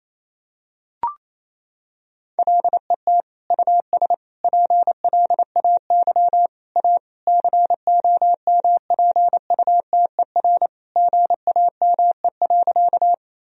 Morse code, heard as a keyed tone, about 28 wpm, 700 Hz. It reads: LET US PLAY A COMPUTER GAME.
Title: Sentences from Top 1000 Words - Review Part 3 - 28wpm